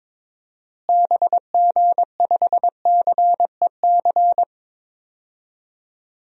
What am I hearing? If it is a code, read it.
Morse code, heard as a keyed tone, about 22 wpm, 700 Hz. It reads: BG5CEC
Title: DXCC-100 Call Signs - 6 Characters - Repeated 3x - 22wpm